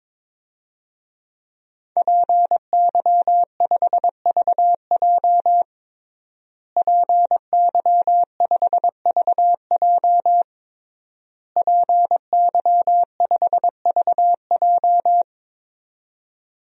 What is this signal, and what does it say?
Morse code, heard as a keyed tone, about 22 wpm, 700 Hz. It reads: PY5VJ PY5VJ PY5VJ
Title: DXCC-100 Call Signs - 5 Characters - Repeated 3x - 22wpm